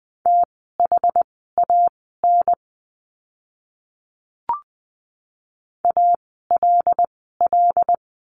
Morse code, heard as a keyed tone, about 20 wpm, spaced as for 14 wpm, 700 Hz. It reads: THAN ALL
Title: 237 Lesson 38f - Letter q - Words 20wpm